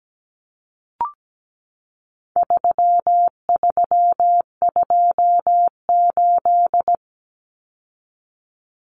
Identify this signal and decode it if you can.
Morse code, heard as a keyed tone, about 17 wpm, 700 Hz. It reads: 3328
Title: Common Cut Numbers - 4-Digit 17wpm